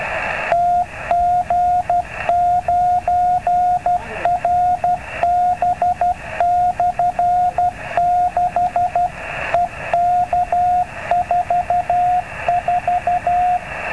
Morse code, heard as a keyed tone, about 12 wpm, 710 Hz, with radio noise. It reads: TG9RB/6 EK44
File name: n6ca / 50MHz / tg9rb